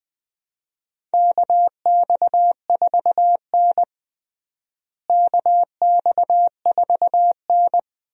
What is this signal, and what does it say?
Morse code, heard as a keyed tone, about 20 wpm, 700 Hz. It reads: KX4N KX4N
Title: USA Call Signs - 4 Characters - Repeated 2x - 20wpm